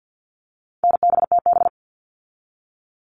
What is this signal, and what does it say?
Morse code, heard as a keyed tone, about 50 wpm, 700 Hz. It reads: NBTB
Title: All Cut Numbers - 4-Digit 50wpm